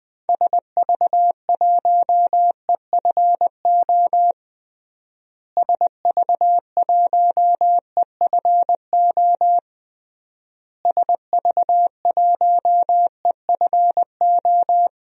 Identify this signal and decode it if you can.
Morse code, heard as a keyed tone, about 20 wpm, 700 Hz. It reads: SV1EFO SV1EFO SV1EFO
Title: DXCC-100 Call Signs - 6 Characters - Repeated 3x - 20wpm